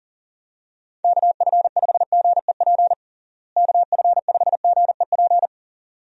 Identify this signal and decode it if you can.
Morse code, heard as a keyed tone, about 40 wpm, 700 Hz. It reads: KF5GEP KF5GEP
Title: USA Call Signs - 6 Characters - Repeated 2x - 40wpm